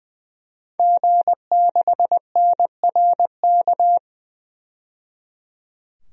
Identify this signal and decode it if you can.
Morse code, heard as a keyed tone, about 21 wpm, 700 Hz. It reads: G6NRK
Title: DXCC-100 Call Signs - 5 Characters - 20wpm